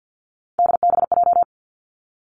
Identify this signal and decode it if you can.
Morse code, heard as a keyed tone, about 50 wpm, 700 Hz. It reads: DBJ
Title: Three Challenging Letter-Number Combinations 50wpm